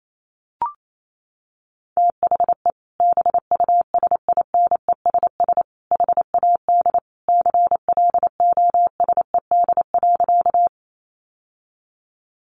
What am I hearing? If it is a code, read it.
Morse code, heard as a keyed tone, about 28 wpm, 700 Hz. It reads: THE BUSINESS HAD CLOSED.
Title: Sentences from Top 1400 Words 28wpm